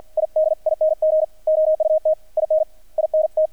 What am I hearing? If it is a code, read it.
Morse code, heard as a keyed tone, about 48 wpm, 610 Hz, with radio noise. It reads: IDING OUT IN INT
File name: archived / CRSnet